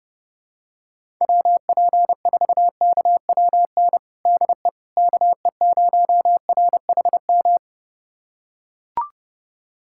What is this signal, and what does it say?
Morse code, heard as a keyed tone, about 30 wpm, 700 Hz. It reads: WP4KWN DE KE0RHM
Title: USA Callsign DE Callsign - 6 Characters - 30wpm